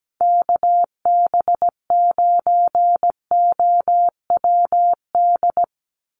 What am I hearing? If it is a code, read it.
Morse code, heard as a keyed tone, about 17 wpm, 700 Hz. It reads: KB9OWD
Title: Call Signs - 17wpm